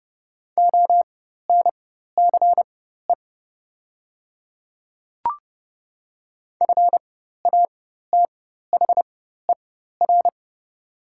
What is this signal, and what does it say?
Morse code, heard as a keyed tone, about 30 wpm, spaced as for 14 wpm, 700 Hz. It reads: ONCE FATHER
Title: 4x Character Spacing - Top 300 Words - 30wpm